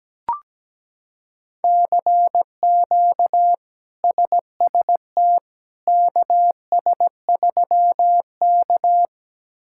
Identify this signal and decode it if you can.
Morse code, heard as a keyed tone, about 17 wpm, 700 Hz. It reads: CQ SST KS3K
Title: K1USN SST QSO Practice - 17wpm